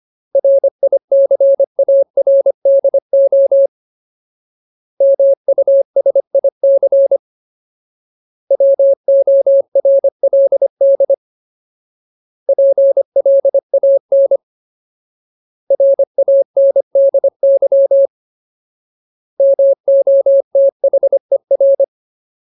Morse code, heard as a keyed tone, about 25 wpm, 550 Hz. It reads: RICARDO MUSIC WORLD PLAN RANDY MOTHER
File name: Ultimate Word List / 25WPM Ultimate